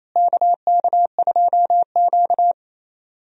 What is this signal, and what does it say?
Morse code, heard as a keyed tone, about 28 wpm, 700 Hz. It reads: KK2Q
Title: USA Call Signs - 4 Characters - Repeated 3x - 28wpm